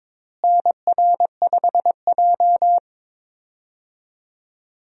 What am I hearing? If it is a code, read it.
Morse code, heard as a keyed tone, about 22 wpm, 700 Hz. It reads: NR5J
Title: USA Call Signs - 4 Characters - Repeated 2x - 22wpm